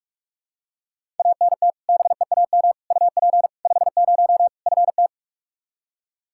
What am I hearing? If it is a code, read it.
Morse code, heard as a keyed tone, about 46 wpm, 700 Hz. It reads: ANT BEAM UP 50 FT
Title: Practice QSO - Exchange 2 - Info - 45wpm